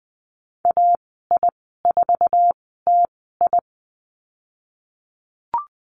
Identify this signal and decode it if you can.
Morse code, heard as a keyed tone, about 20 wpm, spaced as for 14 wpm, 700 Hz. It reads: AI4TI
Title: 055 Lesson 10g - Letter r - Call Signs 20wpm